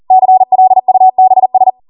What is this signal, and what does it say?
Morse code, heard as a keyed tone, about 40 wpm, 750 Hz. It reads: CLUBS